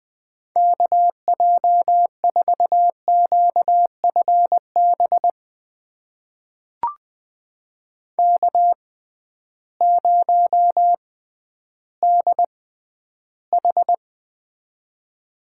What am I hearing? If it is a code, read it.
Morse code, heard as a keyed tone, about 20 wpm, spaced as for 12 wpm, 700 Hz. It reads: KJ4QFB K0DH
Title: USA Call Signs - 6x Character Spacing - 20wpm